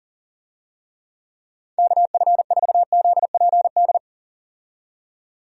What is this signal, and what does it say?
Morse code, heard as a keyed tone, about 40 wpm, 700 Hz. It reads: KF4ZPD
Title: USA Call Signs - 6 Characters - 40wpm